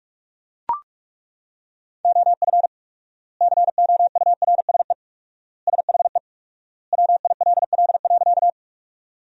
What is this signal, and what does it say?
Morse code, heard as a keyed tone, about 41 wpm, 700 Hz. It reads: OF COURSE SHE WILL.
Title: 4x Word-Spacing - Sentences from Top 500 Words - 45wpm